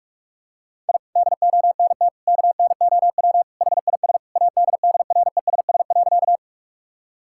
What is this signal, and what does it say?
Morse code, heard as a keyed tone, about 42 wpm, 700 Hz. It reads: I DONT KNOW HIS ADDRESS.
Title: Sentences from Top 1000 Words - Review Part 6 - 45wpm